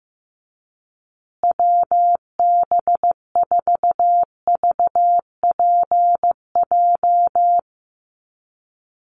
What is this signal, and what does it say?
Morse code, heard as a keyed tone, about 15 wpm, 700 Hz. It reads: WB4VPJ
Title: USA Call Signs Repeated 2x 15wpm